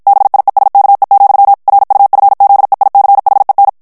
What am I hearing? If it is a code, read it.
Morse code, heard as a keyed tone, about 51 wpm, 800 Hz. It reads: BIESKE, RUFZEICHEN
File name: morse mp3 / ALLEMAND texte clair 2 vitesse 50 mots minute